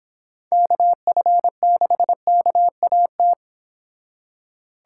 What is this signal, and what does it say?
Morse code, heard as a keyed tone, about 30 wpm, 700 Hz. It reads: KF6KAT